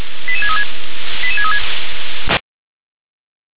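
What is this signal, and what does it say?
Five-tone selective call (ZVEI), heard as quick runs of five tones, tones of about 70 ms. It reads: 88636 88636
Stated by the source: Five-tone ZVEI selective call received by SDR.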